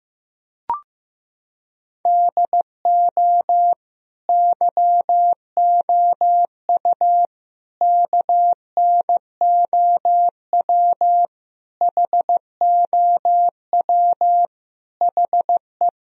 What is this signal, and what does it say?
Morse code, heard as a keyed tone, about 15 wpm, 700 Hz. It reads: DO YOU KNOW HOW HE
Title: Sentences from Top 1000 Words - Review Part 3 - 15wpm